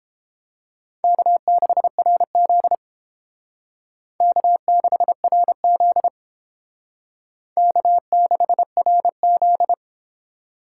Morse code, heard as a keyed tone, about 30 wpm, 700 Hz. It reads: K6RZ K6RZ K6RZ